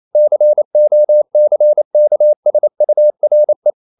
Morse code, heard as a keyed tone, about 28 wpm, 600 Hz. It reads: COCKSURE
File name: CW / Practice / EnglishQuotations0128